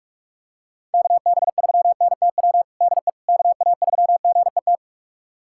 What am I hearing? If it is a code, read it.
Morse code, heard as a keyed tone, about 44 wpm, 700 Hz. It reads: KB3NTW DE KA3GET
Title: USA Callsign DE Callsign - 6 Characters - 45wpm